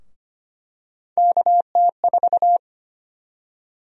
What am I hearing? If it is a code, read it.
Morse code, heard as a keyed tone, about 25 wpm, 700 Hz. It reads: KT4
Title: USA Call Sign Prefixes - 25wpm